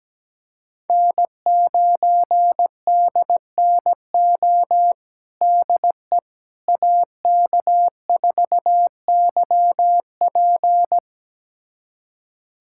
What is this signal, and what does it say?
Morse code, heard as a keyed tone, about 17 wpm, 700 Hz. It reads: N9DNO DE AK4YP
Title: USA Callsign DE Callsign - 5 Characters - 17wpm